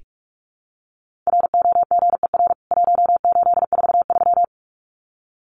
Morse code, heard as a keyed tone, about 44 wpm, 700 Hz. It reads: ROGER 1843
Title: CWT Exchange 45wpm